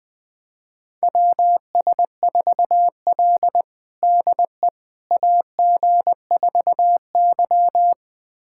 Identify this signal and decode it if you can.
Morse code, heard as a keyed tone, about 20 wpm, 700 Hz. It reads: WS4L DE AG4Y
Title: USA Callsign DE Callsign - 4 Characters - 20wpm